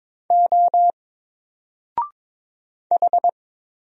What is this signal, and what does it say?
Morse code, heard as a keyed tone, about 22 wpm, 700 Hz. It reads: O H
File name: Single-Letter-Number-22wpm